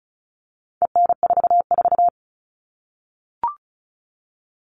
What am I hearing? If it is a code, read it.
Morse code, heard as a keyed tone, about 34 wpm, 700 Hz. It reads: EN44
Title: All Cut Numbers - 4-Digit 35wpm